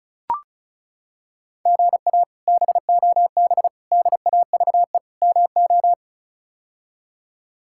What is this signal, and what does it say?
Morse code, heard as a keyed tone, about 36 wpm, 700 Hz. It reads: GA BOB DAVE MO
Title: K1USN SST QSO Practice - 35wpm